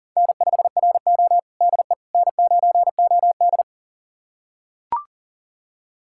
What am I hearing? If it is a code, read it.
Morse code, heard as a keyed tone, about 37 wpm, 700 Hz. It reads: N5RO DE N9OD
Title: USA Callsign DE Callsign - 4 Characters - 40wpm